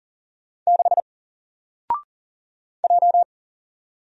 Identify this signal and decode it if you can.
Morse code, heard as a keyed tone, about 40 wpm, 700 Hz. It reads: 6 J
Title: Single Character (Letter:Number:Punctuation:Prosign) 40wpm